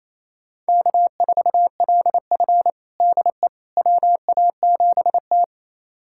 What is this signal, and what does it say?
Morse code, heard as a keyed tone, about 27 wpm, 690 Hz. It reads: K4LF DE WA7T
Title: USA Callsign DE Callsign - 4 Characters - 28wpm